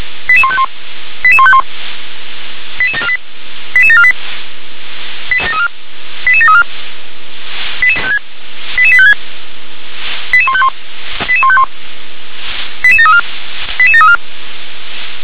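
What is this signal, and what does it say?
Five-tone selective call (ZVEI), heard as quick runs of five tones, tones of about 70 ms. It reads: 88151 88151 88748 88748 88734 88856 88151 88151 88734 88734
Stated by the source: ZVEI five-tone selective call received by SDR.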